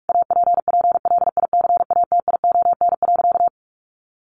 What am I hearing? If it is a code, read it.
Morse code, heard as a keyed tone, about 44 wpm, 700 Hz. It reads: APPLICATION.
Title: Sentences from Top 1200 Words 45wpm